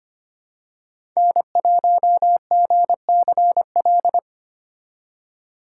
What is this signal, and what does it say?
Morse code, heard as a keyed tone, about 25 wpm, 700 Hz. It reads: N1GCL
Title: USA Call Signs - 5 Characters - 6x Character Spacing - 25wpm